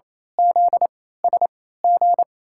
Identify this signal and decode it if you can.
Morse code, heard as a keyed tone, about 28 wpm, spaced as for 16 wpm, 700 Hz. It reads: ZSG
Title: USA Call Signs - 3x Character Spacing - 28wpm